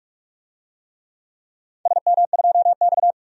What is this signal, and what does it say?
Morse code, heard as a keyed tone, about 46 wpm, 700 Hz. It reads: SM2X
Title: DXCC-100 Call Signs - 4 Characters - 45wpm